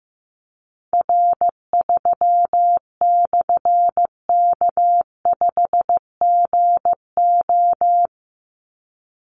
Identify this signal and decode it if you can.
Morse code, heard as a keyed tone, about 15 wpm, 700 Hz. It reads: R3/K5GO
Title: Call Signs Repeated 3x - 15wpm